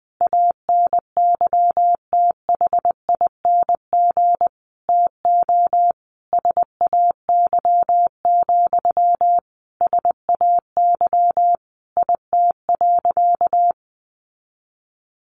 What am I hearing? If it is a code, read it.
Morse code, heard as a keyed tone, about 20 wpm, 700 Hz. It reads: ANYTHING TO SAY, SAY IT.
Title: Sentences from Top 1000 Words 20wpm